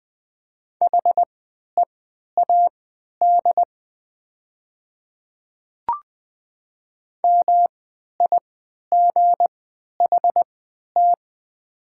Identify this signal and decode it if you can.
Morse code, heard as a keyed tone, about 20 wpm, spaced as for 11 wpm, 700 Hz. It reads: HEAD MIGHT
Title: 3x Character Spacing - Top 300 Words - 20wpm